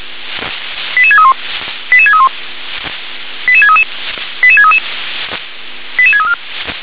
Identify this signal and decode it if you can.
Five-tone selective call (ZVEI), heard as quick runs of five tones, tones of about 70 ms. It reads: 88631 88631 88633 88633 88634